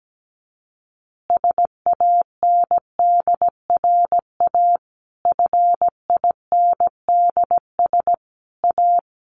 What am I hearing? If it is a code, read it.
Morse code, heard as a keyed tone, about 17 wpm, 700 Hz. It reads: SANDRA FINDS A
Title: Easy Phrases 17wpm